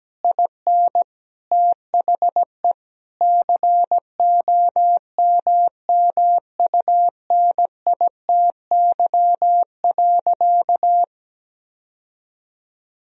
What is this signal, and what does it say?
Morse code, heard as a keyed tone, about 17 wpm, 700 Hz. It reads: IN THE COMMUNITY.
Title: Sentences from Top 1000 Words - Review Part 4 - 17wpm